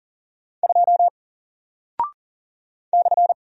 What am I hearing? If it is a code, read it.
Morse code, heard as a keyed tone, about 40 wpm, 700 Hz. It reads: 2 /
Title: Single Character (Letter:Number:Punctuation:Prosign) Alt 40wpm